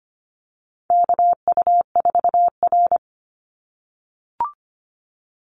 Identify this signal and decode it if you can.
Morse code, heard as a keyed tone, about 25 wpm, 700 Hz. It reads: KU4R
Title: NA SOTA Chaser Callsigns 25wpm